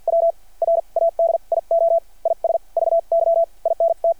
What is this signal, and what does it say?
Morse code, heard as a keyed tone, about 52 wpm, spaced as for 40 wpm, 660 Hz. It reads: W RADIO IS VY INT
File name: archived / CRSnet